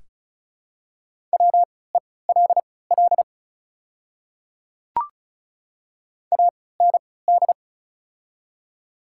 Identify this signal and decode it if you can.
Morse code, heard as a keyed tone, about 36 wpm, spaced as for 20 wpm, 700 Hz. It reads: WELL AND
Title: 3x Character Spacing - Top 300 Words - 35wpm